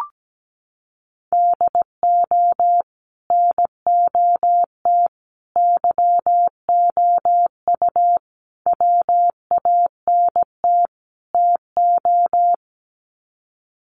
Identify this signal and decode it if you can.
Morse code, heard as a keyed tone, about 17 wpm, 700 Hz. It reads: DO NOT YOU WANT TO
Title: Sets of 5 Words Encore2 17wpm